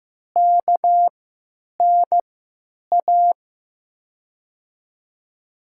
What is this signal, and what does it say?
Morse code, heard as a keyed tone, about 15 wpm, spaced as for 9 wpm, 700 Hz. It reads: KNA
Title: USA Call Signs - 6 Characters - 3x Character Spacing - 15wpm